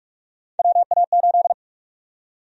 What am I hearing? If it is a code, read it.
Morse code, heard as a keyed tone, about 43 wpm, 700 Hz. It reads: WA8
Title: USA Call Sign Prefixes - Repeated 2x - 45wpm